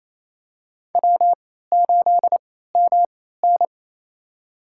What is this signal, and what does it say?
Morse code, heard as a keyed tone, about 28 wpm, spaced as for 16 wpm, 700 Hz. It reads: W8MN
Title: USA Call Signs - 4 Characters - 3x Character Spacing - 28wpm